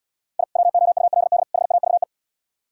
Morse code, heard as a keyed tone, about 74 wpm, 700 Hz. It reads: I WOULD HAVE